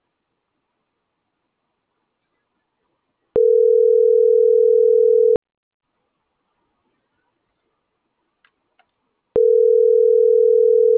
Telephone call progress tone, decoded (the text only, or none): ringback tone